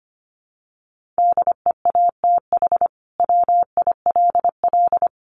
Morse code, heard as a keyed tone, about 25 wpm, 700 Hz. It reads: DEATH WILL
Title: Sentences from Top 1100 Words 25wpm